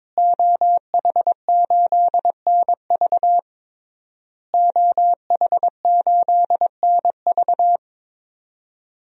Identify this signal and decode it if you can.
Morse code, heard as a keyed tone, about 22 wpm, 700 Hz. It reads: OH8NV OH8NV
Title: DXCC-100 Call Signs - 5 Characters - Repeated 3x - 22wpm